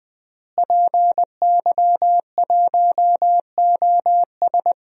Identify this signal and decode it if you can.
Morse code, heard as a keyed tone, about 20 wpm, 700 Hz. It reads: PY1OS
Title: DXCC-100 Call Signs - 5 Characters - Repeated 3x - 20wpm